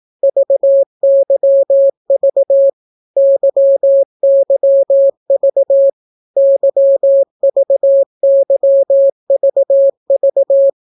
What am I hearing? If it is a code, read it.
Morse code, heard as a keyed tone, about 18 wpm, 550 Hz. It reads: VYV YYV YVYVV